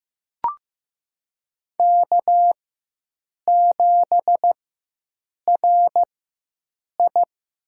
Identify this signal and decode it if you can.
Morse code, heard as a keyed tone, about 15 wpm, spaced as for 7 wpm, 700 Hz. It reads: K7RI
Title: USA Call Signs - 4x Character Spacing - 15wpm